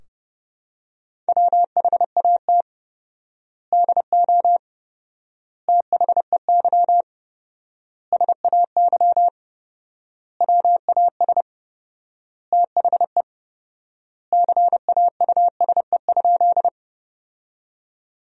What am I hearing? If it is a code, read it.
Morse code, heard as a keyed tone, about 30 wpm, 700 Hz. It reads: WHAT DO THEY SAY WAS THE CAUSE?
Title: 4x Word-Spacing - Sentences from Top 200 Words - 30wpm